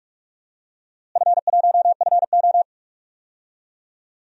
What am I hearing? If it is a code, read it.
Morse code, heard as a keyed tone, about 46 wpm, 700 Hz. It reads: F1FO
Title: DXCC-100 Call Signs - 4 Characters - Repeated 3x - 45wpm